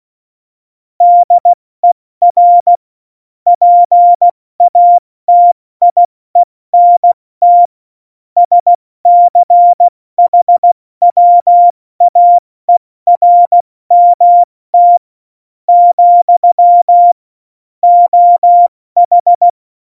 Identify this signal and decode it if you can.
Morse code, heard as a keyed tone, about 16 wpm, spaced as for 14 wpm, 700 Hz. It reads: DER PATIENT SCHWAERMT, OH
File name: Morsekurs / Witze / Witz 08. September 2020, 70 F80